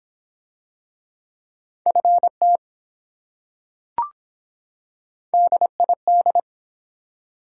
Morse code, heard as a keyed tone, about 23 wpm, 700 Hz. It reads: FT DID